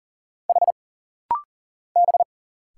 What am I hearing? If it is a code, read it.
Morse code, heard as a keyed tone, about 40 wpm, 700 Hz. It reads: H B
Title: Single Letters - Rapid-Fire 40wpm